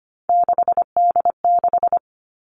Morse code, heard as a keyed tone, about 25 wpm, 700 Hz. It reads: 6D6
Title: Three Challenging Letter-Number Combinations 25wpm